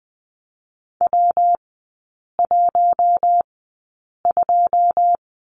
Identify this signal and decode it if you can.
Morse code, heard as a keyed tone, about 20 wpm, spaced as for 8 wpm, 700 Hz. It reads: W12
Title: 114 Lesson 19e - Letter w - Three Characters 20wpm